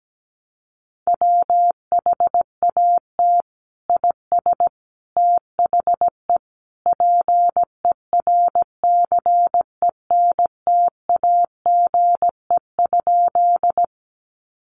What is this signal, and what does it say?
Morse code, heard as a keyed tone, about 17 wpm, 700 Hz. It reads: WHAT IS THE PERCENTAGE?
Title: Sentences from Top 1300 Words 17wpm